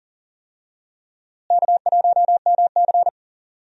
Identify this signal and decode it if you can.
Morse code, heard as a keyed tone, about 40 wpm, 700 Hz. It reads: K1MC
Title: USA Call Signs - 4 Characters - Repeated 3x - 40wpm